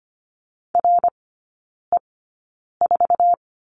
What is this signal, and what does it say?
Morse code, heard as a keyed tone, about 25 wpm, spaced as for 9 wpm, 700 Hz. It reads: RE4
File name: Lesson 14e - Number 2 - Three Characters 25wpm